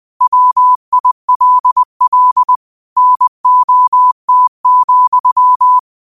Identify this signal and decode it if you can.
Morse code, heard as a keyed tone, about 20 wpm, 1000 Hz. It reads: WILL NOT,